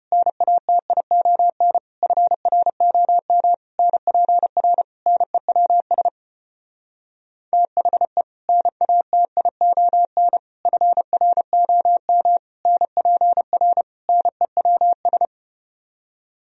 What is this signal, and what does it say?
Morse code, heard as a keyed tone, about 33 wpm, 700 Hz. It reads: NATION FROM NPR NEWS THE NATION FROM NPR NEWS